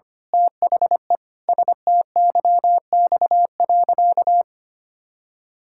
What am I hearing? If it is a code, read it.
Morse code, heard as a keyed tone, about 25 wpm, 700 Hz. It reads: THE STYX.
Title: Pangram Challenge v2 - 25wpm